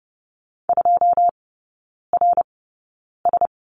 Morse code, heard as a keyed tone, about 30 wpm, spaced as for 9 wpm, 700 Hz. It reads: 2RS
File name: Lesson 16e - Letter c - Three Characters 30wpm